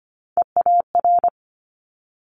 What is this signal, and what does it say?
Morse code, heard as a keyed tone, about 25 wpm, 700 Hz. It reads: EAR
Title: Multi-letter Phonemes Quick Fire 25wpm